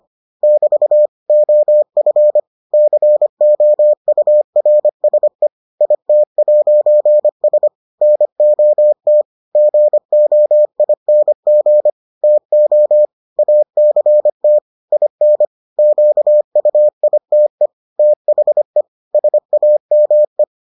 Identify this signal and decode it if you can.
Morse code, heard as a keyed tone, about 25 wpm, 600 Hz. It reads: <BT> OF COURSE IT'S NOT GOING TO ACT IN QUITE THE SAME